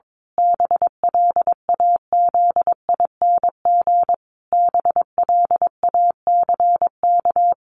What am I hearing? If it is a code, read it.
Morse code, heard as a keyed tone, about 22 wpm, 700 Hz. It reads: BLAZING BLACK
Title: A Sound Of Thunder 22wpm